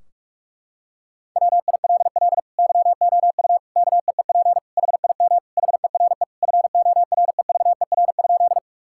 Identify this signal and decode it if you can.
Morse code, heard as a keyed tone, about 46 wpm, 700 Hz. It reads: WILL YOU KEEP HIM HERE FOREVER?
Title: Sentences from Top 1600 Words 45wpm